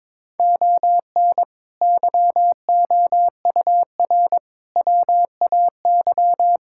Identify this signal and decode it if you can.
Morse code, heard as a keyed tone, about 21 wpm, 700 Hz. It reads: ON YOUR WAY
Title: Sentences from Top 1000 Words - Review Part 6 - 22wpm